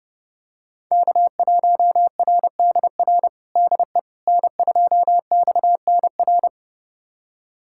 Morse code, heard as a keyed tone, about 30 wpm, 700 Hz. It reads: K1RDR DE N2XNR
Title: USA Callsign DE Callsign - 5 Characters - 30wpm